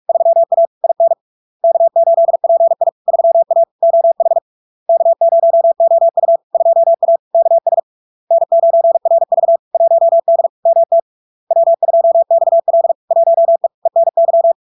Morse code, heard as a keyed tone, about 46 wpm, 670 Hz. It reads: 3A IN K8PI 3A OH K0OU 2A KS N9RV 1D MT W2XL 1E ENY
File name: LARK-FD-2022-Log-in-CW-45wpm-670-Hz